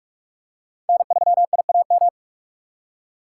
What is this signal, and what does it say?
Morse code, heard as a keyed tone, about 43 wpm, 700 Hz. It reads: N3IAM